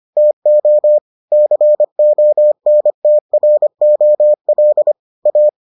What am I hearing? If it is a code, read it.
Morse code, heard as a keyed tone, about 25 wpm, 600 Hz. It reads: TO CONTROL A